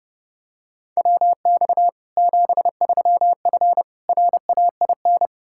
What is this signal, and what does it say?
Morse code, heard as a keyed tone, about 30 wpm, 700 Hz. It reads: WX 73F RAIN
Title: Practice QSO - Exchange 2 - Info - 30wpm